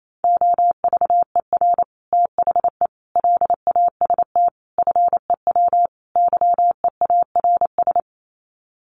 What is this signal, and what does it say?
Morse code, heard as a keyed tone, about 28 wpm, 700 Hz. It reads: OVER THE LAST FEW YEARS
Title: Sets of 5 Words Encore 28wpm